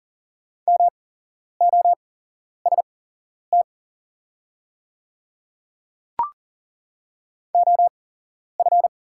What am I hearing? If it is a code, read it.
Morse code, heard as a keyed tone, about 40 wpm, spaced as for 11 wpm, 690 Hz. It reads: MOST OF
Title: 8x Character Spacing - Top 100 Words - 40wpm